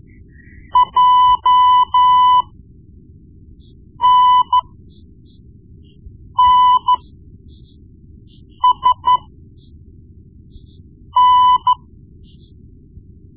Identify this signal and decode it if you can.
Morse code, heard as a keyed tone, about 13 wpm, spaced as for 4 wpm, 1000 Hz. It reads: JNNSN
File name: Morse Lessons / Morse Lessons / Tape-1B